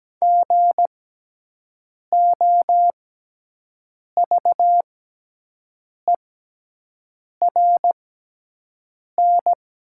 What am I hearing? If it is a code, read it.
Morse code, heard as a keyed tone, about 17 wpm, spaced as for 6 wpm, 700 Hz. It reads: GOVERN
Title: 6x Character Spacing - Top 500 Words - 17wpm